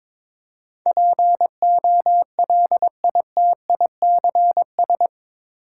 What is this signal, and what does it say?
Morse code, heard as a keyed tone, about 22 wpm, 700 Hz. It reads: POLITICS
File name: practice / 22wpm / Eight Letter Words - 22wpm